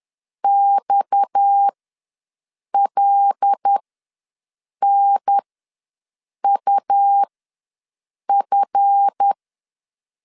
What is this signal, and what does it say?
Morse code, heard as a keyed tone, about 10 wpm, spaced as for 6 wpm, 800 Hz. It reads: XLNUF
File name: LEZIONI / 8 YJ45 / 8 YJ4 ws 10wpm 4020